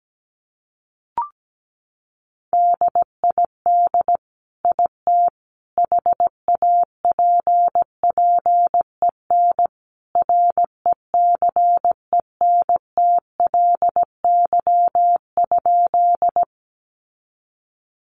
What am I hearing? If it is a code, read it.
Morse code, heard as a keyed tone, about 17 wpm, 700 Hz. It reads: DID IT HAPPEN RECENTLY?
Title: Sentences from Top 1200 Words 17wpm